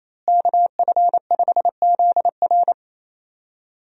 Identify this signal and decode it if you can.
Morse code, heard as a keyed tone, about 27 wpm, 700 Hz. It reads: KF5ZR
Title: USA Call Signs - 5 Characters - Repeated 2x - 28wpm